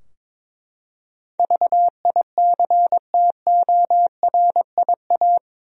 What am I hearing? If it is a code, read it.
Morse code, heard as a keyed tone, about 22 wpm, 700 Hz. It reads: VICTORIA
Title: USA Common Names - 22wpm